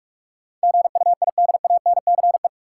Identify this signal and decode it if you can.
Morse code, heard as a keyed tone, about 46 wpm, 700 Hz. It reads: GUIDANCE
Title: Sentences from Top 1600 Words - New Word Drill - 45wpm